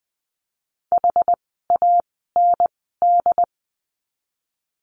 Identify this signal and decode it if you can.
Morse code, heard as a keyed tone, about 20 wpm, 700 Hz. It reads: H A N D
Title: 206 Lesson 33f - Letter j - Words 20wpm